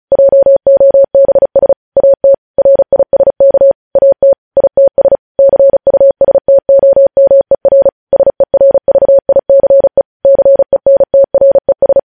Morse code, heard as a keyed tone, about 36 wpm, 550 Hz. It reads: JOBS AT RISK AT ITS CUSTOMER SERVICE CENTRES
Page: News headlines in Morse code at 35 wpm